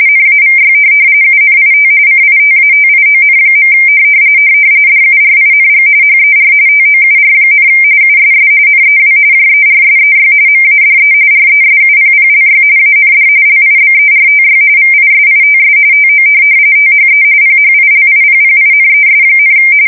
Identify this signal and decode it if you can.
RTTY, about 45.45 baud. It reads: NAVY UNIT CITATION. DURING HER TOUR, USS NEW JERSEY MADE RESUPPLY AND LIBERTY VISITS TO THE PHILIPPINES,
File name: fsk170 / USS-New-Jersey